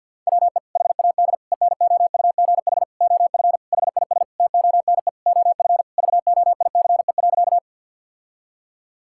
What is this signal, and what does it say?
Morse code, heard as a keyed tone, about 50 wpm, 700 Hz. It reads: WE HAD ENOUGH OF HIS TONE OF VOICE.
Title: Sentences from Top 1600 Words 50wpm